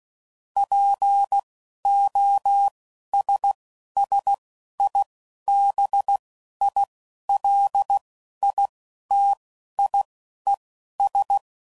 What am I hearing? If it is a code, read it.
Morse code, heard as a keyed tone, about 16 wpm, spaced as for 11 wpm, 800 Hz. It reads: POSSIBILITIES
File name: CW Code Groups @10WPM / Technical Subjects / OR Gate